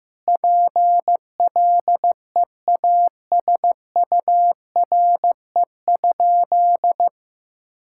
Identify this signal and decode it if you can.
Morse code, heard as a keyed tone, about 16 wpm, 700 Hz. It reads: PLEASURE?
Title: Sentences from Top 1600 Words 15wpm